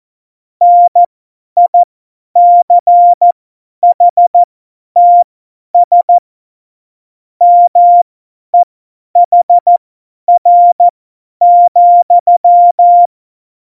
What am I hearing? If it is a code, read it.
Morse code, heard as a keyed tone, about 14 wpm, spaced as for 10 wpm, 700 Hz. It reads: NICHTS MEHR,
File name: Morsekurs / Witze / Witz 27 Juni 17, Tempo 50 F70